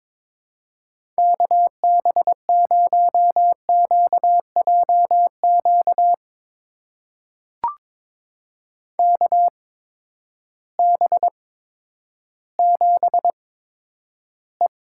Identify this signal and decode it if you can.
Morse code, heard as a keyed tone, about 22 wpm, spaced as for 11 wpm, 700 Hz. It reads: KB0QJQ KB7E
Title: USA Call Signs - 6 Characters - 8x Character Spacing - 22wpm